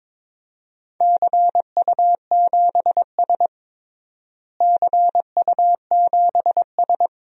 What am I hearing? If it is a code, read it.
Morse code, heard as a keyed tone, about 22 wpm, 700 Hz. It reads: CU7S CU7S
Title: DXCC-100 Call Signs - 4 Characters - Repeated 2x - 22wpm